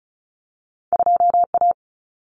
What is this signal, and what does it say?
Morse code, heard as a keyed tone, about 36 wpm, 700 Hz. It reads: 2A
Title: Common Cut Numbers - 2-Digit 35wpm